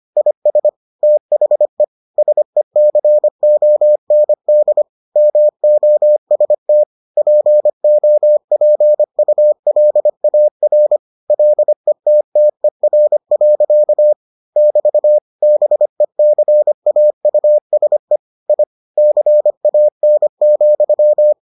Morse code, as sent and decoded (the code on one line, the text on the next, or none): .. ... / - .... / ... -.-. --- -. -.. / -- --- ... - / .--. --- .--. ..- .-.. .- .-. / .-.. - - . .-. .-.-.- / -...- / -... -.-. .- ..- ... / .. / -.-. .- -. --..--
IS THE SECOND MOST POPULAR LETTER. <BT> BECAUSE I CAN,